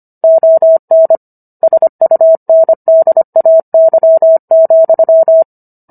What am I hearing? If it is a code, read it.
Morse code, heard as a keyed tone, about 25 wpm, 650 Hz. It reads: ON SUNDAY,